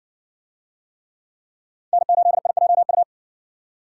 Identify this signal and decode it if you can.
Morse code, heard as a keyed tone, about 60 wpm, 700 Hz. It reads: N8IJU